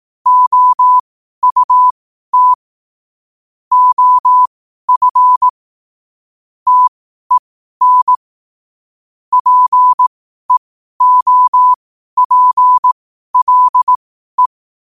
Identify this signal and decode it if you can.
Morse code, heard as a keyed tone, about 18 wpm, spaced as for 12 wpm, 1000 Hz. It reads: OUT OF TEN PEOPLE
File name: qotdfiles / 12wpm